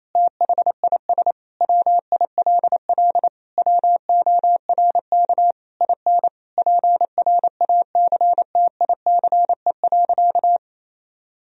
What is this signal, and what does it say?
Morse code, heard as a keyed tone, about 28 wpm, 700 Hz. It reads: THIS WILL WORK IN PRACTICE.
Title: Sentences from Top 1000 Words - Review Part 4 - 28wpm